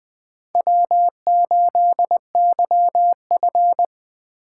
Morse code, heard as a keyed tone, about 20 wpm, 700 Hz. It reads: W8YF
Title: USA Call Signs - 4 Characters - 8x Character Spacing - 20wpm